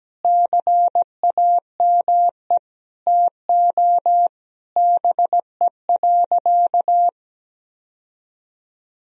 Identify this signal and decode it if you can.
Morse code, heard as a keyed tone, about 17 wpm, 700 Hz. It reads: CAME TO BE.
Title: Sentences from Top 1000 Words - Review Part 3 - 17wpm